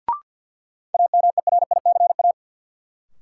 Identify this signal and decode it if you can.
Morse code, heard as a keyed tone, about 52 wpm, 700 Hz. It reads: AMERICA
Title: Sets of 5 Words - New Word Drill - 50wpm